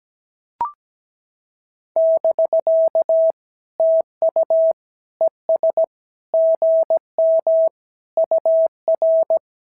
Morse code, heard as a keyed tone, about 16 wpm, 650 Hz. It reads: <BK> TU ES GM UR
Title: SOTA QSO Practice 17wpm